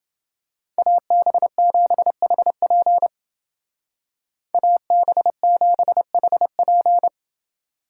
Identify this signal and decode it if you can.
Morse code, heard as a keyed tone, about 28 wpm, 700 Hz. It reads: AB7HP AB7HP